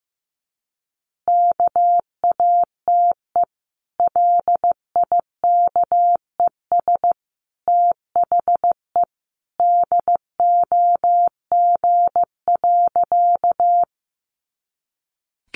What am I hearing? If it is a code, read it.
Morse code, heard as a keyed tone, about 15 wpm, 700 Hz. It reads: KATE LIKES THE DOG.
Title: Easy Phrases 15wpm